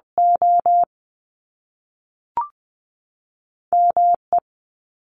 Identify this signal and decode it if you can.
Morse code, heard as a keyed tone, about 20 wpm, spaced as for 5 wpm, 700 Hz. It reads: OME
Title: US States 20wpm